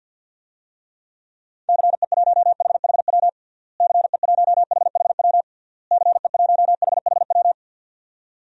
Morse code, heard as a keyed tone, about 51 wpm, 700 Hz. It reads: CE1HHW CE1HHW CE1HHW
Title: DXCC-100 Call Signs - 6 Characters - Repeated 3x - 50wpm